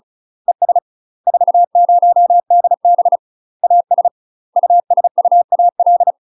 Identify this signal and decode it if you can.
Morse code, heard as a keyed tone, about 36 wpm, 700 Hz. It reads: ES 40DB AS USUAL
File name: UK MASTER QSOs TRAINING / 35WPM / UK MASTER QSOs 0018